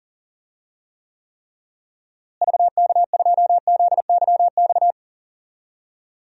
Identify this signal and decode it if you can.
Morse code, heard as a keyed tone, about 40 wpm, 700 Hz. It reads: VK2ZYX
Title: DXCC-100 Call Signs - 6 Characters - Repeated 3x - 40wpm